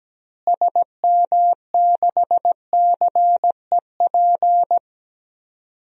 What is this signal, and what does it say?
Morse code, heard as a keyed tone, about 17 wpm, 700 Hz. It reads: SM6CEP
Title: DXCC-100 Call Signs - 6 Characters - Repeated 3x - 17wpm